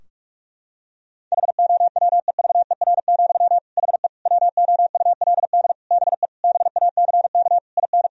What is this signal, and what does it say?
Morse code, heard as a keyed tone, about 47 wpm, 700 Hz. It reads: HOWEVER, HE WOULD BE BACK IN
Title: Sentences from Top 1000 Words - Review Part 2 - 45wpm